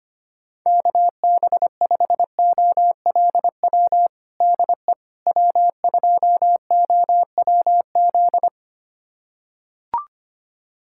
Morse code, heard as a keyed tone, about 25 wpm, 700 Hz. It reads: KB5OLW DE W2OWZ
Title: USA Callsign DE Callsign 25wpm